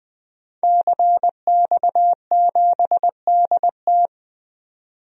Morse code, heard as a keyed tone, about 20 wpm, 700 Hz. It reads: CX7DT
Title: DXCC-100 Call Signs - 5 Characters - 20wpm